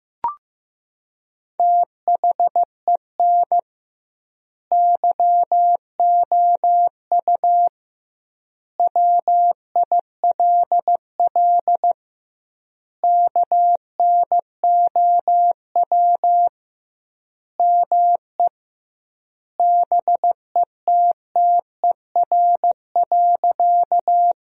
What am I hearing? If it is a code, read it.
Morse code, heard as a keyed tone, about 16 wpm, 700 Hz. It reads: THEN YOU WILL KNOW ME BETTER.
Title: 2x Word-Spacing - Sentences from Top 400 Words - 15wpm